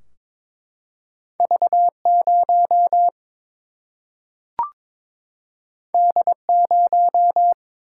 Morse code, heard as a keyed tone, about 22 wpm, 700 Hz. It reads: V0 D0
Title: Single-Letter Single-Number 22wpm